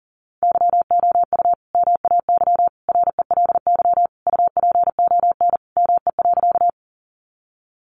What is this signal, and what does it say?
Morse code, heard as a keyed tone, about 40 wpm, 700 Hz. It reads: YOU MAY RELY UPON ME.
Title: Sentences from Top 1200 Words 40wpm